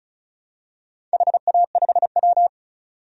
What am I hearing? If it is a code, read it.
Morse code, heard as a keyed tone, about 36 wpm, 700 Hz. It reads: HA5W
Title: DXCC-100 Call Signs - 4 Characters - 35wpm